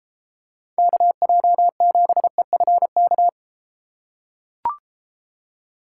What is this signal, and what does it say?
Morse code, heard as a keyed tone, about 31 wpm, 700 Hz. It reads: KJ7EFK